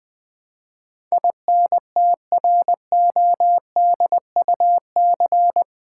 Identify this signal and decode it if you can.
Morse code, heard as a keyed tone, about 20 wpm, 700 Hz. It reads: INTRODUC
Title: Build A Word - 4 Letters At A Time - 12-Letter Words - 20wpm